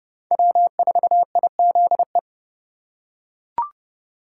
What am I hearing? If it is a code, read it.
Morse code, heard as a keyed tone, about 30 wpm, 700 Hz. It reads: W4IZE
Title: USA Call Signs - 5 Characters - 4x Character Spacing - 30wpm